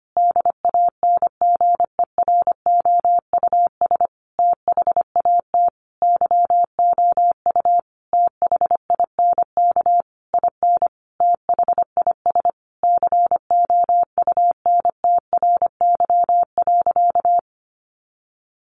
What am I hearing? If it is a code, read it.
Morse code, heard as a keyed tone, about 25 wpm, 700 Hz. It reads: DANGEROUS THAT YOU THINK IN THIS COUNTRY.
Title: Sentences from Top 1200 Words 25wpm